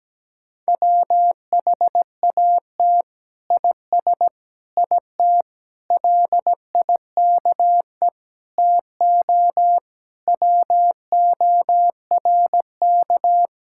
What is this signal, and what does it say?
Morse code, heard as a keyed tone, about 17 wpm, 700 Hz. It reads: WHAT IS IT LIKE TO WORK
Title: Sentences from Top 1000 Words - Review Part 6 - 17wpm